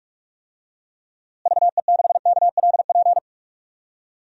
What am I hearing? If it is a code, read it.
Morse code, heard as a keyed tone, about 46 wpm, 700 Hz. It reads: VE6KLP
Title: DXCC-100 Call Signs - 6 Characters - Repeated 2x - 45wpm